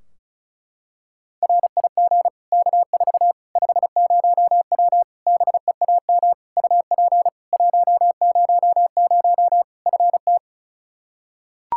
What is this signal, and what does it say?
Morse code, heard as a keyed tone, about 36 wpm, 700 Hz. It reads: RIG K4 50W BEAM UP 100 FT
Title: Practice QSO - Exchange 2 - Info - 35wpm